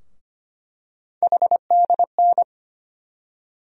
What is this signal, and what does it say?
Morse code, heard as a keyed tone, about 25 wpm, 700 Hz. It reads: HDN
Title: USA Call Sign Postfixes - 3 Letters 25wpm